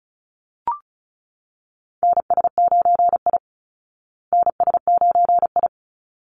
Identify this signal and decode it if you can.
Morse code, heard as a keyed tone, about 34 wpm, 700 Hz. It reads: NS9I NS9I
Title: Call Signs Repeated 3x - 35wpm